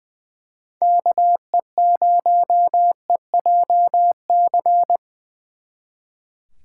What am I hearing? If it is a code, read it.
Morse code, heard as a keyed tone, about 24 wpm, 700 Hz. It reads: KE0EJC
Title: Speed Racing - USA Call Signs 20wpm